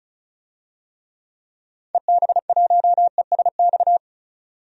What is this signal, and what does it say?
Morse code, heard as a keyed tone, about 34 wpm, 700 Hz. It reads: EB1ESX
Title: DXCC-100 Call Signs - 6 Characters - 35wpm